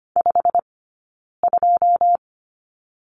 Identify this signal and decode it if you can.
Morse code, heard as a keyed tone, about 25 wpm, 700 Hz. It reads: 5 2
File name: course / 25:10wpm / 092 Lesson 16d - Letter c - Two Characters 25wpm